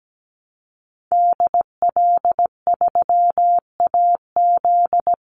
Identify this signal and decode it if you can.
Morse code, heard as a keyed tone, about 17 wpm, 700 Hz. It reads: DL3AZ
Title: Call Signs - 17wpm